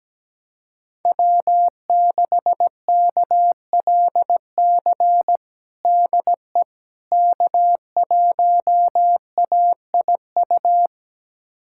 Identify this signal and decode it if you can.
Morse code, heard as a keyed tone, about 17 wpm, 700 Hz. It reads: W6KLC DE K1AIU
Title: USA Callsign DE Callsign - 5 Characters - 17wpm